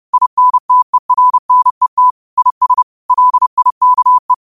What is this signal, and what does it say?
Morse code, heard as a keyed tone, about 30 wpm, 1000 Hz. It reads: INTERNET IS LIKE
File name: qotdfiles / 30wpm